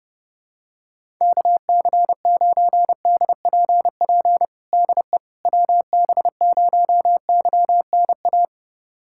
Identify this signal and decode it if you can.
Morse code, heard as a keyed tone, about 30 wpm, 690 Hz. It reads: KC9DPP DE WB0YNA
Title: USA Callsign DE Callsign - 6 Characters - 30wpm